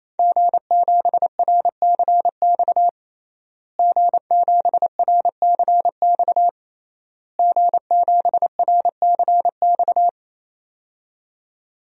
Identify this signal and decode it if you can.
Morse code, heard as a keyed tone, about 28 wpm, 700 Hz. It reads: G7RCX G7RCX G7RCX
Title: DXCC-100 Call Signs - 5 Characters - Repeated 3x - 28wpm